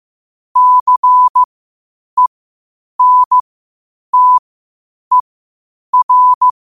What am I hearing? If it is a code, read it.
Morse code, heard as a keyed tone, about 15 wpm, spaced as for 8 wpm, 1000 Hz. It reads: CENTER
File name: qotdfiles / 8wpm